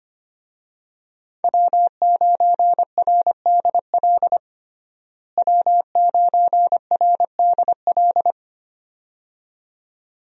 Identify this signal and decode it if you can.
Morse code, heard as a keyed tone, about 25 wpm, 700 Hz. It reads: W9RDL W9RDL
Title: USA Call Signs - 5 Characters - Repeated 2x - 25wpm